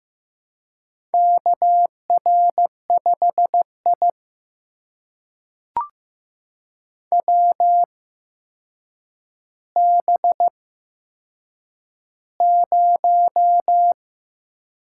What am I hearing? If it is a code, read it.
Morse code, heard as a keyed tone, about 15 wpm, spaced as for 7 wpm, 700 Hz. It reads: KR5I WB0
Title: USA Call Signs - 4 Characters - 8x Character Spacing - 15wpm